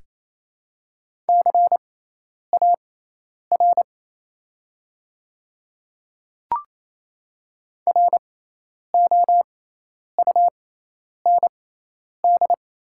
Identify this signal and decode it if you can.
Morse code, heard as a keyed tone, about 27 wpm, spaced as for 10 wpm, 700 Hz. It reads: CAR ROUND
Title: 6x Character Spacing - Top 300 Words - 28wpm